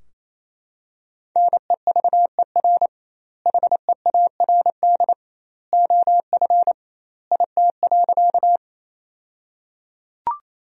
Morse code, heard as a keyed tone, about 27 wpm, 700 Hz. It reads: NEVER HEARD OF IT.
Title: 2x Word-Spacing - Sentences from Top 400 Words - 28wpm